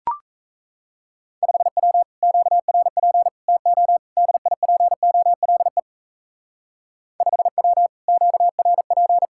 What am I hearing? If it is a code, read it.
Morse code, heard as a keyed tone, about 32 wpm, 700 Hz. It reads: 5W QRP TO DIPOLE 5W QRP
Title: Speed-Racing Common QSO Phrases Encore - 28wpm